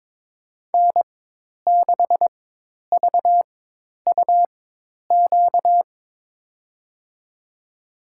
Morse code, heard as a keyed tone, about 22 wpm, spaced as for 10 wpm, 700 Hz. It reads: N6VUQ
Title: USA Call Signs - 5 Characters - 4x Character Spacing - 22wpm